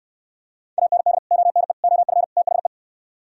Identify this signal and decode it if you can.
Morse code, heard as a keyed tone, about 67 wpm, 700 Hz. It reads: AND ONE OF THE